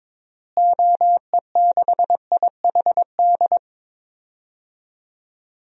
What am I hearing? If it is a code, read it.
Morse code, heard as a keyed tone, about 22 wpm, 700 Hz. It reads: OE6IHD
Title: DXCC-100 Call Signs - 6 Characters - Repeated 2x - 22wpm